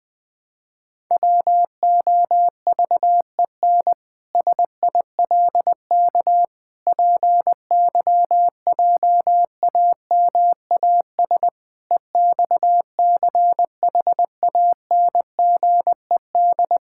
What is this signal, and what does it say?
Morse code, heard as a keyed tone, about 20 wpm, 700 Hz. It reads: WOVEN SILK PYJAMAS EXCHANGED
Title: Pangram Challenge v2 - 20wpm